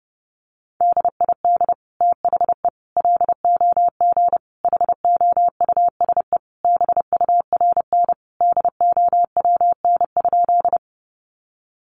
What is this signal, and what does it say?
Morse code, heard as a keyed tone, about 30 wpm, 700 Hz. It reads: DID THE LOG HOUSE BURN DOWN?
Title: Sentences from Top 1400 Words 30wpm